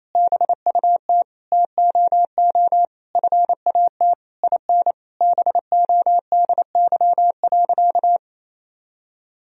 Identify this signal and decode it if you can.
Morse code, heard as a keyed tone, about 28 wpm, 700 Hz. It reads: BUT TOO FAT IN BODY.
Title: Sentences from Top 1000 Words - Review Part 4 - 28wpm